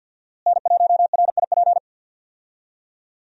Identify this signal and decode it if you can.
Morse code, heard as a keyed tone, about 50 wpm, 700 Hz. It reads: N1RIP
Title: USA Call Signs - 5 Characters - Repeated 3x - 50wpm